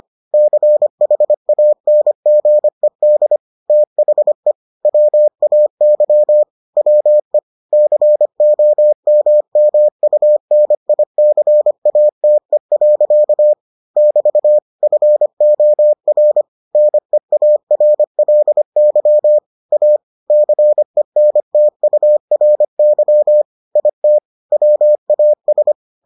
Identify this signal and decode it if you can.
Morse code, heard as a keyed tone, about 25 wpm, 600 Hz. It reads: CHANGED THE WAY WE COMMUNICATE. <BT> FOR NEARLY A CENTURY IT WAS